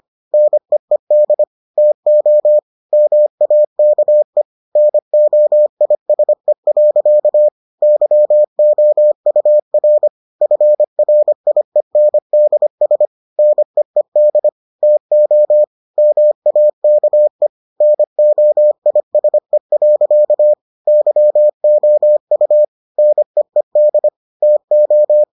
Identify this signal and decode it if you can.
Morse code, heard as a keyed tone, about 25 wpm, 600 Hz. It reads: NEED TO MAKE NOISE. YOUR FRIENDS NEED TO MAKE NOISE. YOU NEED TO